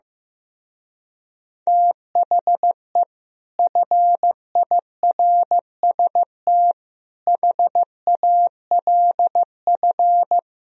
Morse code, heard as a keyed tone, about 15 wpm, 700 Hz. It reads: THE FIRST HALF
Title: Speed-Racing - Sets of 5 Words 15wpm